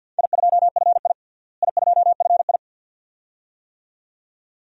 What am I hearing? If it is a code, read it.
Morse code, heard as a keyed tone, about 48 wpm, 700 Hz. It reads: I2FI I2FI
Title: DXCC-100 Call Signs - 4 Characters - Repeated 2x - 50wpm